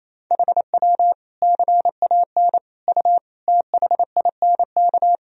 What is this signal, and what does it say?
Morse code, heard as a keyed tone, about 28 wpm, 700 Hz. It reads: HW CAN U THINK
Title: Sentences from Top 200 Words and Abbreviations 28wpm